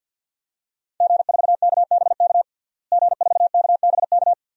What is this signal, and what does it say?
Morse code, heard as a keyed tone, about 49 wpm, 700 Hz. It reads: G4KBK G4KBK
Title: DXCC-100 Call Signs - 5 Characters - Repeated 2x - 50wpm